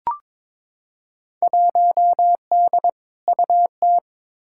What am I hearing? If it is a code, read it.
Morse code, heard as a keyed tone, about 22 wpm, 700 Hz. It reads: 1D UT
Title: Field Day Exchanges - 22wpm